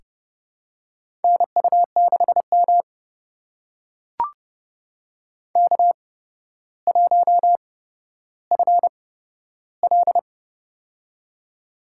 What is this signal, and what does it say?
Morse code, heard as a keyed tone, about 31 wpm, spaced as for 13 wpm, 700 Hz. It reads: NU6M K1FL
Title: USA Call Signs - 4 Characters - 8x Character Spacing - 30wpm